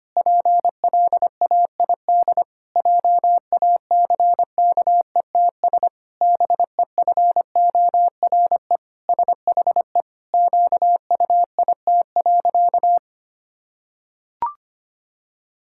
Morse code, heard as a keyed tone, about 25 wpm, 700 Hz. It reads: PLAID JACKETS BEFORE SHE QUIT.
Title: Pangram Challenge v2 - 25wpm